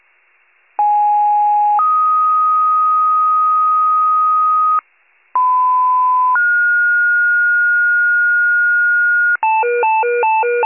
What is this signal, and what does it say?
Two-tone page, heard as a one-second tone, then a three-second tone, about 830 then 1250 Hz, about 1010 then 1430 Hz, with noise. It is heard from 0.8 to 9.4 s.